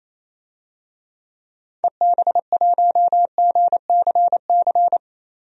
Morse code, heard as a keyed tone, about 28 wpm, 700 Hz. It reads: EB1GCC
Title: DXCC-100 Call Signs - 6 Characters - 28wpm